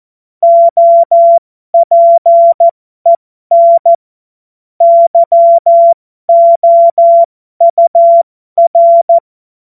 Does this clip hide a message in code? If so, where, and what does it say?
Yes: Morse code from 0.4 to 9.2 s, heard as a keyed tone, about 14 wpm, spaced as for 12 wpm, 680 Hz. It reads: OPEN YOUR